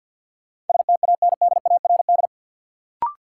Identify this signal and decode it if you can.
Morse code, heard as a keyed tone, about 52 wpm, 700 Hz. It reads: STANDARD
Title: Sentences from Top 1200 Words - New Word Drill Alt - 50wpm